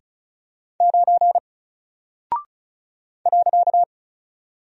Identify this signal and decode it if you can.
Morse code, heard as a keyed tone, about 35 wpm, 700 Hz. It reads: 9 .
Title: Single Character (Letter:Number:Punctuation:Prosign) 35wpm